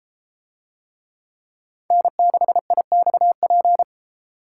Morse code, heard as a keyed tone, about 31 wpm, 700 Hz. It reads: N6IXP